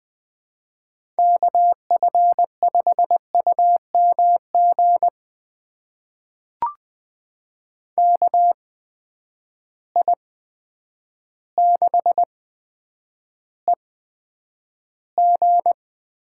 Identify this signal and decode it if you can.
Morse code, heard as a keyed tone, about 20 wpm, spaced as for 9 wpm, 700 Hz. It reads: KF5UMG KI6EG
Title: USA Call Signs - 8x Character Spacing - 20wpm